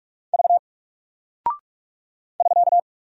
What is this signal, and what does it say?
Morse code, heard as a keyed tone, about 46 wpm, 700 Hz. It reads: V <SK>
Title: Single Character (Letter:Number:Punctuation:Prosign) Alt 45wpm